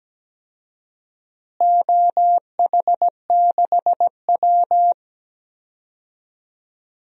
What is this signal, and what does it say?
Morse code, heard as a keyed tone, about 17 wpm, 700 Hz. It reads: OH6W
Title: DXCC-100 Call Signs - 4 Characters - Repeated 2x - 17wpm